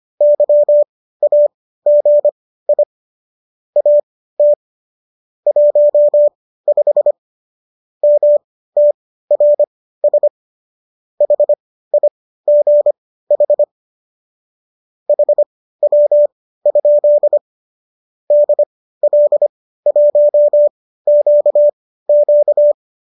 Morse code, heard as a keyed tone, about 25 wpm, spaced as for 15 wpm, 590 Hz. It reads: YAGI AT 15 MTRS HIGH HW? DL1QQ